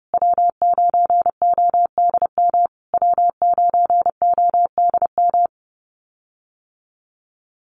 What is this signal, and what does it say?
Morse code, heard as a keyed tone, about 30 wpm, 700 Hz. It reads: W9ODM W9ODM
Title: USA Call Signs Repeated 3x 30wpm